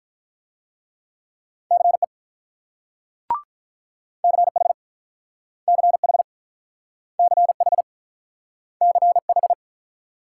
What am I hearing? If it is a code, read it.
Morse code, heard as a keyed tone, about 46 wpm, 690 Hz. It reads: CE CH CH CH CH